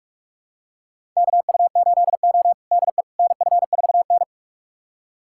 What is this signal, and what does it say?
Morse code, heard as a keyed tone, about 46 wpm, 700 Hz. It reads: KU8O DE NF4N
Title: USA Callsign DE Callsign - 4 Characters - 45wpm